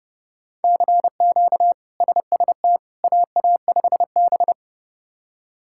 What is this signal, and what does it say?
Morse code, heard as a keyed tone, about 30 wpm, 700 Hz. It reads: CQ SST AA5B
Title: K1USN SST QSO Practice - 30wpm